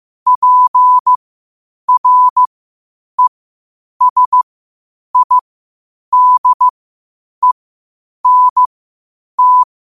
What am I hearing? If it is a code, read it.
Morse code, heard as a keyed tone, about 15 wpm, spaced as for 8 wpm, 1000 Hz. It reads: PRESIDENT
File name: qotdfiles / 8wpm